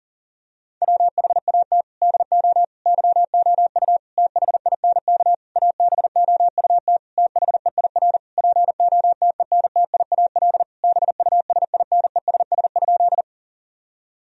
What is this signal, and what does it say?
Morse code, heard as a keyed tone, about 40 wpm, 700 Hz. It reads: WHAT DO YOU THINK ABOUT THEIR POTENTIAL BUSINESS?
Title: Sentences from Top 1000 Words - Review Part 4 - 40wpm